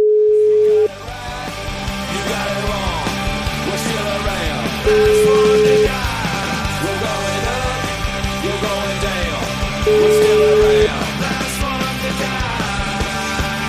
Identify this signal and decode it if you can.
Ringback tone (Europe, 425 Hz), with noise.